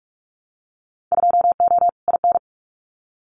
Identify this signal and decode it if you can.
Morse code, heard as a keyed tone, about 44 wpm, 700 Hz. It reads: 2O IN
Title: Winter Field Day Exchanges 45wpm